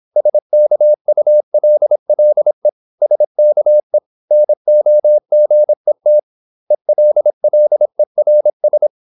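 Morse code, heard as a keyed tone, about 26 wpm, 600 Hz. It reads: SKULLE SKE NOGET ELLERS
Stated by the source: High-speed (QRQ) Morse at 26 wpm.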